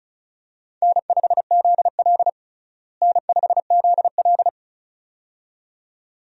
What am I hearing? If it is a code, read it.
Morse code, heard as a keyed tone, about 36 wpm, 700 Hz. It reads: N5ZL N5ZL
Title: USA Call Signs - 4 Characters - Repeated 2x - 35wpm